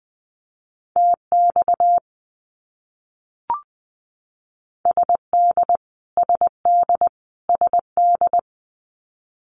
Morse code, heard as a keyed tone, about 20 wpm, 700 Hz. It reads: TX SD SD SD
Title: US States Repeated 3x 20wpm